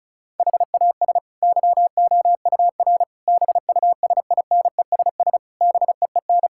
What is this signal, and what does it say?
Morse code, heard as a keyed tone, about 36 wpm, 700 Hz. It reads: HAS YOUR BUSINESS BEEN
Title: Sentences from Top 1000 Words - Review Part 6 - 35wpm